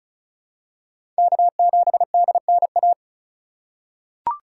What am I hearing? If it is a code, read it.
Morse code, heard as a keyed tone, about 34 wpm, 700 Hz. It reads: K7DNA
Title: USA Call Signs - 5 Characters - 4x Character Spacing - 35wpm